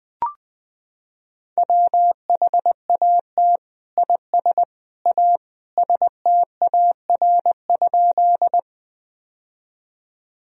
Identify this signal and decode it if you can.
Morse code, heard as a keyed tone, about 20 wpm, 700 Hz. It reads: WHAT IS A STAR?
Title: Sentences from Top 1000 Words - Review Part 2 - 20wpm